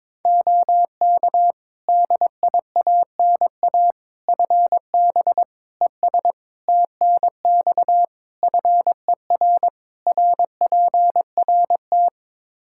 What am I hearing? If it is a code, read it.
Morse code, heard as a keyed tone, about 22 wpm, 700 Hz. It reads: OK DIANA FB ES TNX FER RPRT
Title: Practice QSO - Exchange 2 - Info - 22wpm